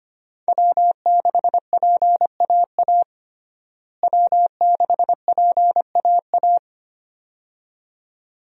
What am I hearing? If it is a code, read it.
Morse code, heard as a keyed tone, about 25 wpm, 700 Hz. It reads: W6PAA W6PAA
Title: USA Call Signs - 5 Characters - Repeated 2x - 25wpm